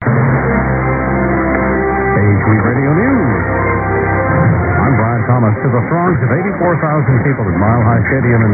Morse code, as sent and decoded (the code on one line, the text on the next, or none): none